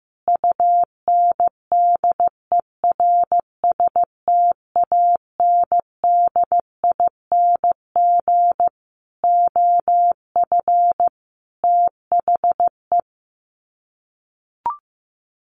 Morse code, heard as a keyed tone, about 15 wpm, 700 Hz. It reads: UNDERSTANDING OF THE
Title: Sets of 5 Words Encore2 15wpm